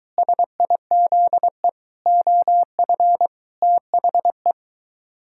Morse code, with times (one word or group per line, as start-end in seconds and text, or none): 0.2-1.7 SIZE
2.1-3.3 OF
3.6-4.5 THE